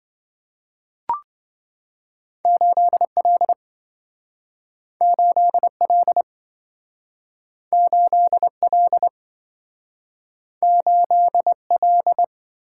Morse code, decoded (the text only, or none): 8L 8L 8L 8L